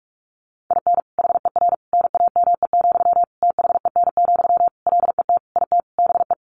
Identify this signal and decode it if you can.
Morse code, heard as a keyed tone, about 44 wpm, 700 Hz. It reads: IN HER NAME, THEN, LET IT BE
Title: Sentences from Top 500 Words 45wpm